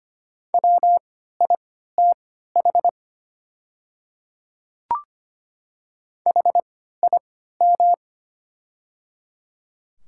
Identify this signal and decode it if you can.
Morse code, heard as a keyed tone, about 25 wpm, spaced as for 14 wpm, 700 Hz. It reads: WITH HIM
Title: 3x Character Spacing - Top 300 Words - 25wpm